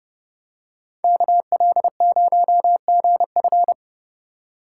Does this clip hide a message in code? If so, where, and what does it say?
Yes: Morse code from 1.0 to 3.7 s, heard as a keyed tone, about 29 wpm, 700 Hz. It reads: KL0GF